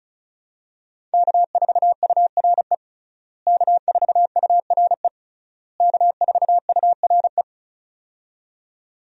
Morse code, heard as a keyed tone, about 36 wpm, 700 Hz. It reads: K4URE K4URE K4URE
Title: USA Call Signs - 5 Characters - Repeated 3x - 35wpm